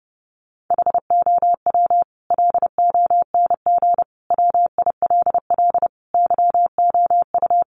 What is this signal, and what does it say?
Morse code, heard as a keyed tone, about 30 wpm, 700 Hz. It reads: HOW LONG WILL YOU
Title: Sentences from Top 1200 Words 30wpm